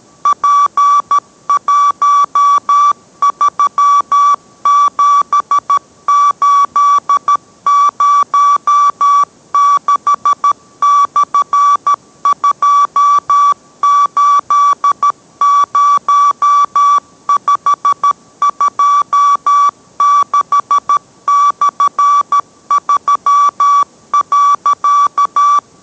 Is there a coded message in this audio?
Yes: Morse code from 0.3 to 25.6 s, heard as a keyed tone, about 13 wpm, 1150 Hz. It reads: P137806/280526/3.